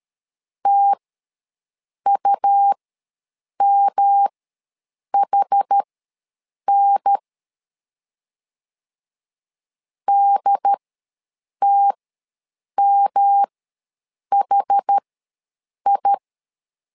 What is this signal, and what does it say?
Morse code, heard as a keyed tone, about 13 wpm, spaced as for 7 wpm, 800 Hz. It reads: TUMHN DTMHI
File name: LEZIONI / 3 MWRO / 3 M ws 12wpm 1745